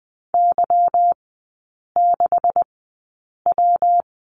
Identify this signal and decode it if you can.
Morse code, heard as a keyed tone, about 20 wpm, spaced as for 8 wpm, 690 Hz. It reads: Y6W
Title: 145 Lesson 24e - Letter y - Three Characters 20wpm